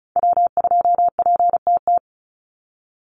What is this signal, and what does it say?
Morse code, heard as a keyed tone, about 34 wpm, 700 Hz. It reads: W2PTT
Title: Call Signs - 35wpm